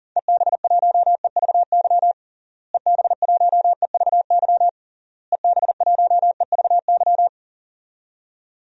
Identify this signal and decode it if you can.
Morse code, heard as a keyed tone, about 40 wpm, 700 Hz. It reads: EB1EVY EB1EVY EB1EVY
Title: DXCC-100 Call Signs - 6 Characters - Repeated 3x - 40wpm